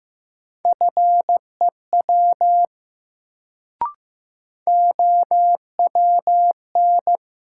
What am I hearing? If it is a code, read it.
Morse code, heard as a keyed tone, about 15 wpm, 700 Hz. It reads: FEW OWN
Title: Sentences from Top 300 Words - New Word Drill - 15wpm